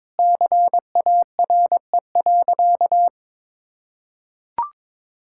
Morse code, heard as a keyed tone, about 21 wpm, 700 Hz. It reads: CARE.
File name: practice / 22wpm / Sentences with 4-Word Phrases - 22wpm